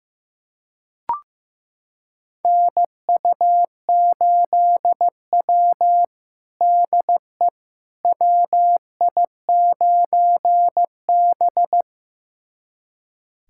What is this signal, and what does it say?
Morse code, heard as a keyed tone, about 16 wpm, 700 Hz. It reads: NU8W DE WI9B
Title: USA Callsign DE Callsign - 4 Characters - 15wpm